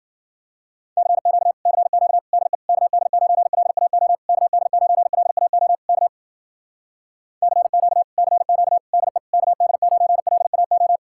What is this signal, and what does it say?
Morse code, heard as a keyed tone, about 53 wpm, 700 Hz. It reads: CQ CQ DE KD9LAO KD9LAO K CQ CQ DE KD9LAO